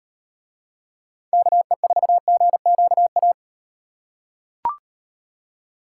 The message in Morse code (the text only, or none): KE4GQA